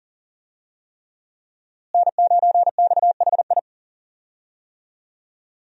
Morse code, heard as a keyed tone, about 40 wpm, 700 Hz. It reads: N9XHI
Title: USA Call Signs - 5 Characters - Repeated 3x - 40wpm